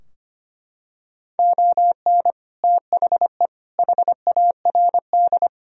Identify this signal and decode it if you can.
Morse code, heard as a keyed tone, about 25 wpm, 700 Hz. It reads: ON THE HARD
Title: Sentences from Top 1000 Words - Review Part 2 - 25wpm